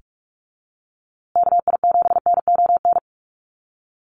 Morse code, heard as a keyed tone, about 44 wpm, 700 Hz. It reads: KI7NON
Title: USA Call Signs Repeated 3x 45wpm